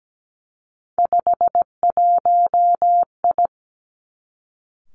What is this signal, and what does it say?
Morse code, heard as a keyed tone, about 17 wpm, 700 Hz. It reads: 51I
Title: Three Challenging Letter-Number Combinations 17wpm